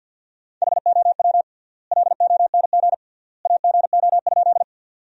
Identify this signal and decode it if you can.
Morse code, heard as a keyed tone, about 53 wpm, 700 Hz. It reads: HOW LONG AGO?
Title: 3x Word-Spacing - Sentences from Top 500 Words - 50wpm